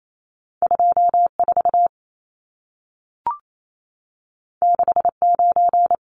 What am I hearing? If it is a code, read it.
Morse code, heard as a keyed tone, about 28 wpm, 700 Hz. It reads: 24 69
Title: Numbers - 2-Digit 28wpm